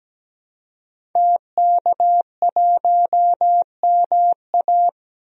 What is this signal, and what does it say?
Morse code, heard as a keyed tone, about 17 wpm, 700 Hz. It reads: TK1MA
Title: DXCC-100 Call Signs - 5 Characters - 17wpm